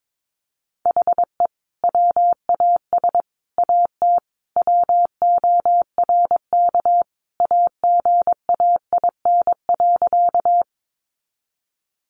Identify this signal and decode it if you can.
Morse code, heard as a keyed tone, about 22 wpm, 700 Hz. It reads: HE WAS AT WORK AGAIN.
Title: Sentences from Top 200 Words 22wpm